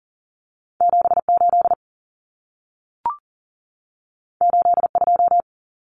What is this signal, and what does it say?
Morse code, heard as a keyed tone, about 40 wpm, 700 Hz. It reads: 78 82
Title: Numbers - 2-Digit 40wpm